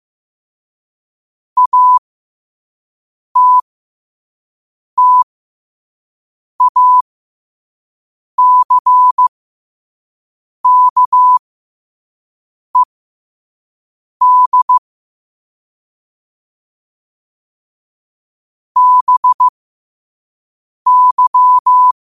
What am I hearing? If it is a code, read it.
Morse code, heard as a keyed tone, about 15 wpm, spaced as for 5 wpm, 1000 Hz. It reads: ATTACKED BY
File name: qotdfiles / 5wpm